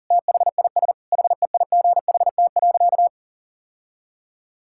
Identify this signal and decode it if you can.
Morse code, heard as a keyed tone, about 40 wpm, 700 Hz. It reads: THIS HEIGHT.
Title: Sentences from Top 1600 Words 40wpm